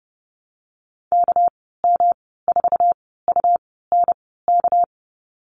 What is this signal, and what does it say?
Morse code, heard as a keyed tone, about 30 wpm, spaced as for 17 wpm, 700 Hz. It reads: KM4UNK
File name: Lesson 37g - Letter x - Call Signs 30wpm